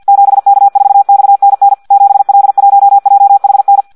Morse code, heard as a keyed tone, about 50 wpm, 800 Hz. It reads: 7K4XNN 7L1JHN
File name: Audio practica CW / qrq-tx-50 2